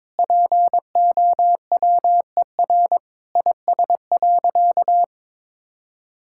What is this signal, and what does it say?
Morse code, heard as a keyed tone, about 21 wpm, 700 Hz. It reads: POWER IS.
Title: Sentences from Top 1000 Words - Review Part 6 - 22wpm